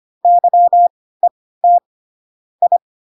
Morse code, heard as a keyed tone, about 25 wpm, spaced as for 16 wpm, 700 Hz. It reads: YET I
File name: Pinocchio / Pinocchio ch2 10 25 ik1hge